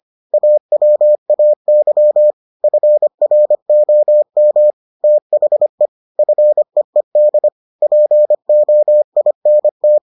Morse code, heard as a keyed tone, about 25 wpm, 600 Hz. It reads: AWAY FROM THE FEED POINT